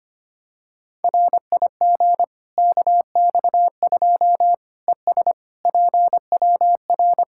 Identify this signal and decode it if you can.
Morse code, heard as a keyed tone, about 25 wpm, 700 Hz. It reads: RIG KX2 ES PWR
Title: Practice QSO - Exchange 2 - Info - 25wpm